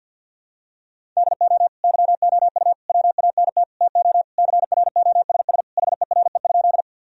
Morse code, heard as a keyed tone, about 52 wpm, 700 Hz. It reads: DO YOU WANT TO CROSS HERE?
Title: Sentences from Top 1000 Words - Review Part 1 - 50wpm